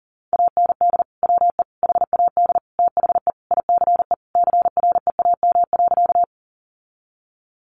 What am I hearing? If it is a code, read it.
Morse code, heard as a keyed tone, about 40 wpm, 700 Hz. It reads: AND WE HAD THE ICE CREAM.
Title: Sentences from Top 1500 Words 40wpm